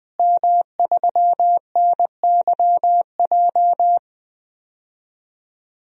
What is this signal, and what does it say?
Morse code, heard as a keyed tone, about 20 wpm, 700 Hz. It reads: M3NYJ
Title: DXCC-100 Call Signs - 5 Characters - Repeated 2x - 20wpm